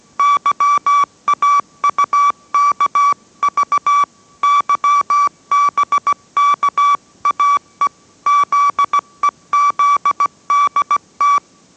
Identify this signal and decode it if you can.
Morse code, heard as a keyed tone, about 17 wpm, 1150 Hz. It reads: YAUKV YBKAE ZEZDT